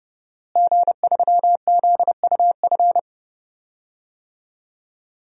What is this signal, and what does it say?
Morse code, heard as a keyed tone, about 30 wpm, 700 Hz. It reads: G3ZUF
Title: DXCC-100 Call Signs - 5 Characters - Repeated 2x - 30wpm